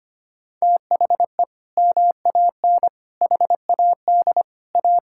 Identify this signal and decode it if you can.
Morse code, heard as a keyed tone, about 25 wpm, 700 Hz. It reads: THE MAN HAD A
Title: Sentences from Top 1000 Words - Review Part 6 - 25wpm